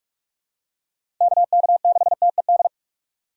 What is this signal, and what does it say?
Morse code, heard as a keyed tone, about 46 wpm, 700 Hz. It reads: KK6TED
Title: USA Call Signs - 6 Characters - 2x Character Spacing - 45wpm